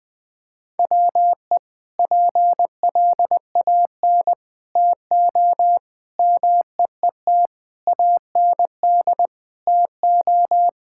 Morse code, heard as a keyed tone, about 20 wpm, 700 Hz. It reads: WE PLAN TO MEET AND TO
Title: Sentences with 2-Word Phrases - 20wpm